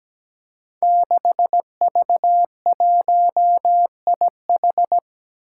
Morse code, heard as a keyed tone, about 17 wpm, 700 Hz. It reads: 6V1IH
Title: Five Challenging Letter-Number Combinations 17wpm